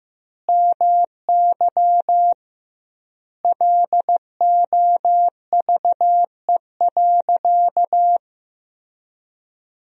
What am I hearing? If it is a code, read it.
Morse code, heard as a keyed tone, about 15 wpm, 700 Hz. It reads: MY LOVE.
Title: 2x Word-Spacing - Sentences from Top 400 Words - 15wpm